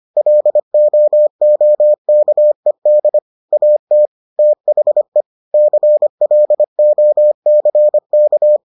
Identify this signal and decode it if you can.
Morse code, heard as a keyed tone, about 25 wpm, 600 Hz. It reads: LOOKED AT THE CLOCK